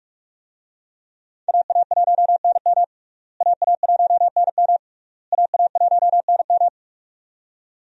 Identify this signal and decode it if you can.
Morse code, heard as a keyed tone, about 46 wpm, 700 Hz. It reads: AA1NM AA1NM AA1NM
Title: USA Call Signs - 5 Characters - Repeated 3x - 45wpm